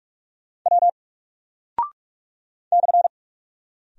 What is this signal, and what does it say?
Morse code, heard as a keyed tone, about 44 wpm, 700 Hz. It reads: W /
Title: Single Character (Letter:Number:Punctuation:Prosign) 45wpm